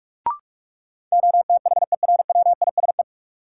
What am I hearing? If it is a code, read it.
Morse code, heard as a keyed tone, about 46 wpm, 700 Hz. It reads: OTHERWISE
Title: Sentences from Top 1200 Words - New Word Drill Alt - 45wpm